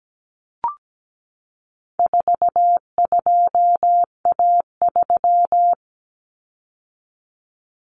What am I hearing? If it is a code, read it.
Morse code, heard as a keyed tone, about 17 wpm, 700 Hz. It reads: 42A3
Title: Common Cut Numbers - 4-Digit 17wpm